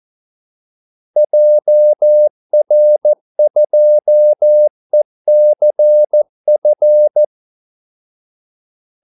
Morse code, heard as a keyed tone, about 14 wpm, 600 Hz. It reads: JR2ECF
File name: Callsigns-14wpm